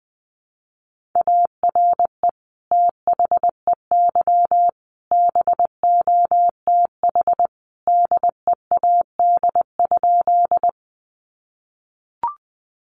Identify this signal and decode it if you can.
Morse code, heard as a keyed tone, about 20 wpm, 700 Hz. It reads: ARE THEY BOTH DEAD?
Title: Sentences from Top 1500 Words 20wpm